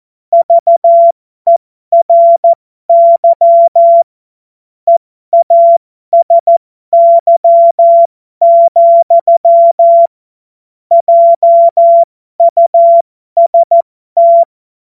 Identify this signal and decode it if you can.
Morse code, heard as a keyed tone, about 14 wpm, spaced as for 12 wpm, 680 Hz. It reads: VERY EASY, JUST
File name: Morsekurs / Witze